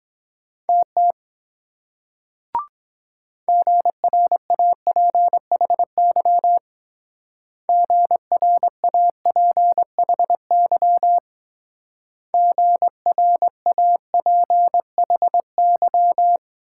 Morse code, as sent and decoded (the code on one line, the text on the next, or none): - - / --. .-. .- .--. .... -.-- / --. .-. .- .--. .... -.-- / --. .-. .- .--. .... -.--
TT GRAPHY GRAPHY GRAPHY